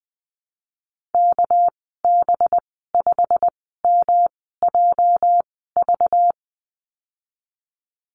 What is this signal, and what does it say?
Morse code, heard as a keyed tone, about 20 wpm, spaced as for 15 wpm, 700 Hz. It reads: KB5MJV
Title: 213 Lesson 34g - Number 8 - Call Signs 20wpm